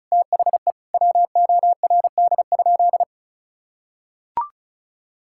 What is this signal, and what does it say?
Morse code, heard as a keyed tone, about 36 wpm, 700 Hz. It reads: THE WORD?
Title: Sentences from Top 1000 Words - Review Part 6 - 35wpm